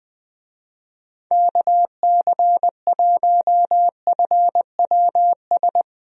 Morse code, heard as a keyed tone, about 20 wpm, 700 Hz. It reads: KC1FWS
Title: USA Call Signs - 6 Characters - Repeated 2x - 20wpm